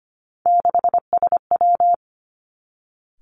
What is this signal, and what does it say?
Morse code, heard as a keyed tone, about 25 wpm, 700 Hz. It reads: 6SW
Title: Three Challenging Letter-Number Combinations 25wpm